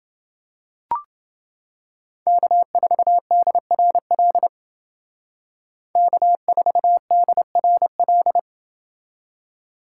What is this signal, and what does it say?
Morse code, heard as a keyed tone, about 28 wpm, 700 Hz. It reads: K4DRL K4DRL